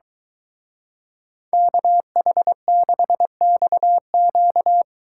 Morse code, heard as a keyed tone, about 23 wpm, 700 Hz. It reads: KH6XQ